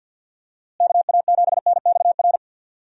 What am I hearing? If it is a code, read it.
Morse code, heard as a keyed tone, about 52 wpm, 700 Hz. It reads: KA7NXR
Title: USA Call Signs - 6 Characters - Repeated 2x - 50wpm